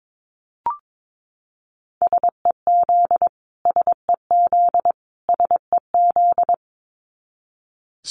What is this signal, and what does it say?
Morse code, heard as a keyed tone, about 22 wpm, 700 Hz. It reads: SEZ SEZ SEZ
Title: Abbreviations Repeated 3x - 22wpm